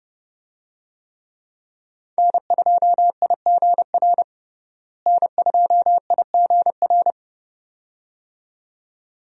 Morse code, heard as a keyed tone, about 31 wpm, 700 Hz. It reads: N2IGR N2IGR
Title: USA Call Signs - 5 Characters - Repeated 2x - 30wpm